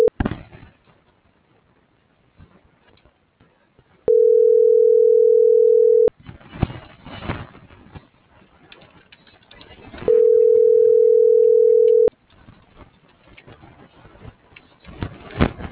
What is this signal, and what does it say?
Ringback tone (North America), 440 + 480 Hz.